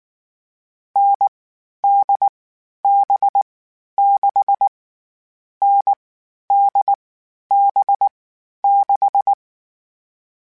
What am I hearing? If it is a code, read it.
Morse code, heard as a keyed tone, about 19 wpm, 800 Hz. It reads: N D B 6 N D B 6
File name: Short Runs / Short Runs-mp3s / NDB NDB6 @18